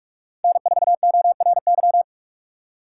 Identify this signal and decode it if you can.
Morse code, heard as a keyed tone, about 44 wpm, 700 Hz. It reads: N4ORY